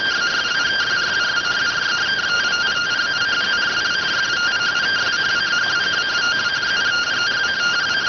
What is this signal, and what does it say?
Two-tone teleprinter signal, about 170 Hz shift.